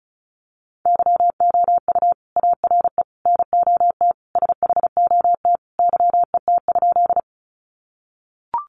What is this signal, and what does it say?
Morse code, heard as a keyed tone, about 34 wpm, 700 Hz. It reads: YOU ARE NOT SHOT YET?
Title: Sentences from Top 1100 Words 35wpm